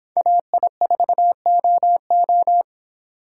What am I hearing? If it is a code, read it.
Morse code, heard as a keyed tone, about 29 wpm, 700 Hz. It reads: AI4OO